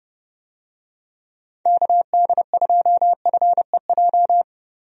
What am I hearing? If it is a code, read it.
Morse code, heard as a keyed tone, about 30 wpm, 700 Hz. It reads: KD2FEJ
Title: USA Call Signs - 6 Characters - Repeated 2x - 30wpm